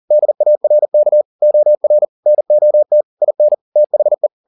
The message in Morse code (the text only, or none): DARK OR NOT IN THE